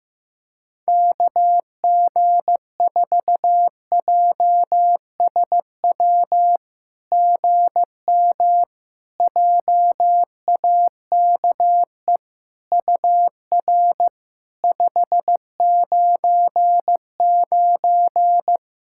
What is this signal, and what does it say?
Morse code, heard as a keyed tone, about 15 wpm, 700 Hz. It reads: KG4JSW GM JAKE UR 599
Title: SOTA QSO Practice 15wpm